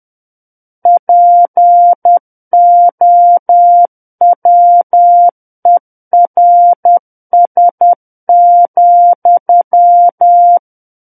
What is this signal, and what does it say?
Morse code, heard as a keyed tone, about 10 wpm, 700 Hz. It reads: POWERS,
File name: Book One / 01-Chapter 1 - 10 WPM